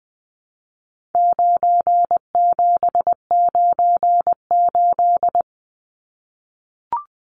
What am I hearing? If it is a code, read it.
Morse code, heard as a keyed tone, about 20 wpm, 700 Hz. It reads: 9798
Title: Numbers - 4-Digit 20wpm